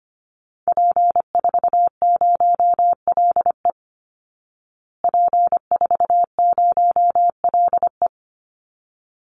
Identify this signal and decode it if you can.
Morse code, heard as a keyed tone, about 25 wpm, 700 Hz. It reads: P40LE P40LE
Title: Call Signs Repeated 3x - 25wpm